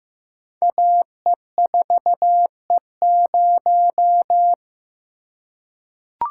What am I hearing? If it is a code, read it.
Morse code, heard as a keyed tone, about 15 wpm, 700 Hz. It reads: AE4E0
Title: Code Groups - 5 Letter:Numbers - 15wpm